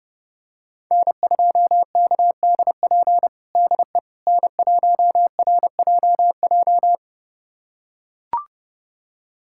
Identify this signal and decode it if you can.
Morse code, heard as a keyed tone, about 32 wpm, 700 Hz. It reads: N2KDP DE N1RJJ
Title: USA Callsign DE Callsign - 5 Characters - 30wpm